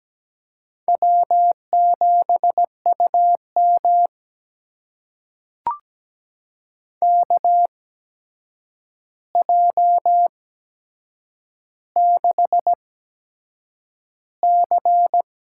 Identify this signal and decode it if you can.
Morse code, heard as a keyed tone, about 17 wpm, spaced as for 7 wpm, 700 Hz. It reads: W7UM KJ6C
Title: USA Call Signs - 8x Character Spacing - 17wpm